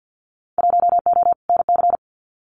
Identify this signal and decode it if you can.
Morse code, heard as a keyed tone, about 50 wpm, 700 Hz. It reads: 1O NC
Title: Winter Field Day Exchanges 50wpm